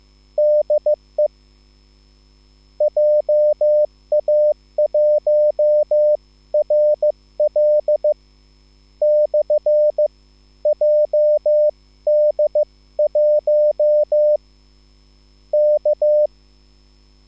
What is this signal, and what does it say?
Morse code, heard as a keyed tone, about 15 wpm, spaced as for 12 wpm, 600 Hz. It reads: DE JA1RL/JD1 K
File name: QSL / CW lesson